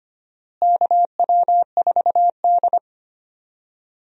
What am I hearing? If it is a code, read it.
Morse code, heard as a keyed tone, about 25 wpm, 700 Hz. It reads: KW4D
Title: USA Call Signs - 4 Characters - Repeated 3x - 25wpm